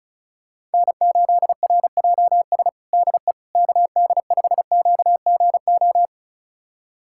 Morse code, heard as a keyed tone, about 34 wpm, 700 Hz. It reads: N8RJS DE KD5QGO
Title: USA Callsign DE Callsign 35wpm